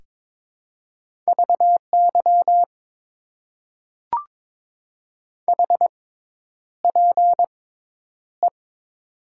Morse code, heard as a keyed tone, about 22 wpm, spaced as for 10 wpm, 700 Hz. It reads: VY HPE
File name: Top 90 QSO Elements - 6x Character Spacing - 22wpm